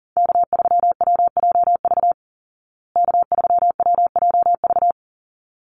Morse code, heard as a keyed tone, about 40 wpm, 700 Hz. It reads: K3WJV K3WJV
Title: Call Signs Repeated 3x - 40wpm